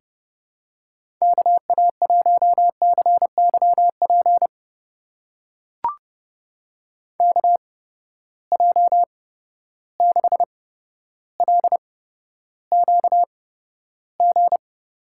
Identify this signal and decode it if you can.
Morse code, heard as a keyed tone, about 30 wpm, spaced as for 13 wpm, 700 Hz. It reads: KA1CYP KJ6LQG
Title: USA Call Signs - 6 Characters - 8x Character Spacing - 30wpm